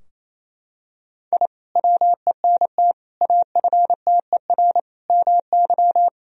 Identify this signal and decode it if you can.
Morse code, heard as a keyed tone, about 28 wpm, 700 Hz. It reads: I WENT AFTER MY
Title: Sentences from Top 1000 Words - Review Part 2 - 28wpm